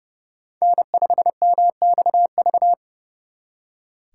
Morse code, heard as a keyed tone, about 38 wpm, 700 Hz. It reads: N5MXV